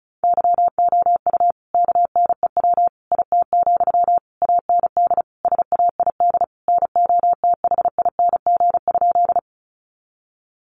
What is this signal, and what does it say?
Morse code, heard as a keyed tone, about 34 wpm, 700 Hz. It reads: YOU KNEW IT, AND SAID NOTHING?
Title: Sentences from Top 500 Words 35wpm